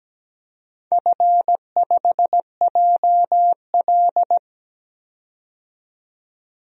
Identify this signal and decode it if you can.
Morse code, heard as a keyed tone, about 17 wpm, 700 Hz. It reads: F5JL
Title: DXCC-100 Call Signs - 4 Characters - 17wpm